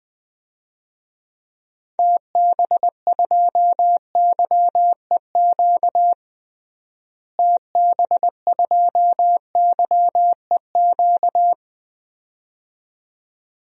Morse code, heard as a keyed tone, about 20 wpm, 700 Hz. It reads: TB2YEQ TB2YEQ
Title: DXCC-100 Call Signs - 6 Characters - Repeated 2x - 20wpm